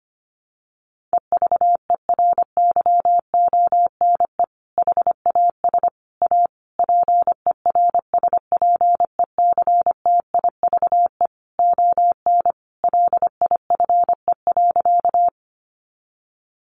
Morse code, heard as a keyed tone, about 25 wpm, 700 Hz. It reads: EVERYONE HAS A PERSPECTIVE ON LIFE.
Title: Sentences from Top 1200 Words 25wpm